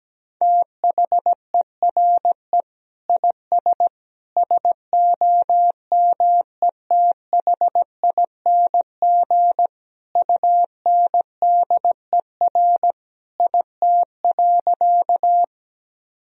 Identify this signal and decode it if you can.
Morse code, heard as a keyed tone, about 17 wpm, 700 Hz. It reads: THERE IS SOMETHING UNDER IT.
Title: Sentences from Top 1000 Words - Review Part 1 - 17wpm